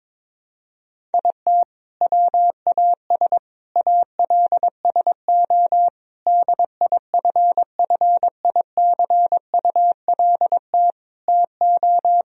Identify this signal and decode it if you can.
Morse code, heard as a keyed tone, about 22 wpm, 700 Hz. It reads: IT WAS ALSO DIFFICULT TO
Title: Sentences from Top 1000 Words - Review Part 1 - 22wpm